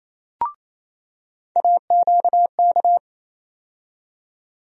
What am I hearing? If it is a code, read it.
Morse code, heard as a keyed tone, about 28 wpm, 700 Hz. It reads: AQK
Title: USA Call Sign Postfixes - 3 Letters 28wpm